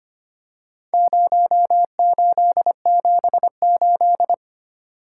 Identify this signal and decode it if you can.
Morse code, heard as a keyed tone, about 25 wpm, 700 Hz. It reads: 0878
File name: practice / 25wpm / Four Letter-Number Combinations - 25wpm